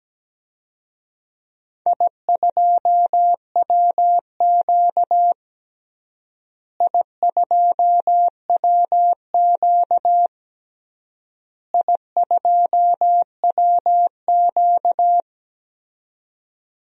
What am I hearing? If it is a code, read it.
Morse code, heard as a keyed tone, about 17 wpm, 700 Hz. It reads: I2WQ I2WQ I2WQ
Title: DXCC-100 Call Signs - 4 Characters - Repeated 3x - 17wpm